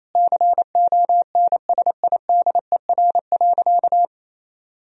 Morse code, heard as a keyed tone, about 28 wpm, 700 Hz. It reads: CONSIDER.
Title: Sentences from Top 1000 Words - Review Part 4 - 28wpm